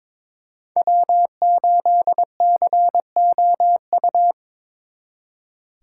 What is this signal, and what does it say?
Morse code, heard as a keyed tone, about 26 wpm, 700 Hz. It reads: W8COU
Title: Speed Racing - USA 1x3 Call Signs 22wpm